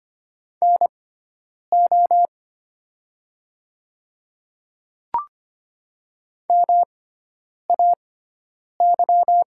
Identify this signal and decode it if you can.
Morse code, heard as a keyed tone, about 25 wpm, spaced as for 9 wpm, 700 Hz. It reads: NO MAY
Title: 6x Character Spacing - Top 100 Words - 25wpm